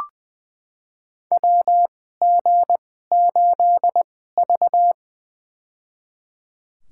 Morse code, heard as a keyed tone, about 21 wpm, spaced as for 15 wpm, 700 Hz. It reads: WG8V
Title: USA Call Signs - 2x Character Spacing - 20wpm